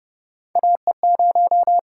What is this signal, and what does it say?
Morse code, heard as a keyed tone, about 30 wpm, 700 Hz. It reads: AE0
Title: USA Call Signs - 5 Characters - 6x Character Spacing - 30wpm